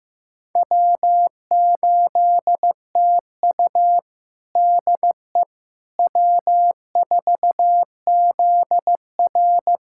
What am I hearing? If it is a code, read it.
Morse code, heard as a keyed tone, about 15 wpm, 700 Hz. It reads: W8TU DE W4ZR
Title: USA Callsign DE Callsign - 4 Characters - 15wpm